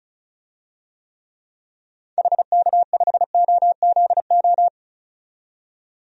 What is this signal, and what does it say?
Morse code, heard as a keyed tone, about 36 wpm, 700 Hz. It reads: HK5OZO
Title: DXCC-100 Call Signs - 6 Characters - 35wpm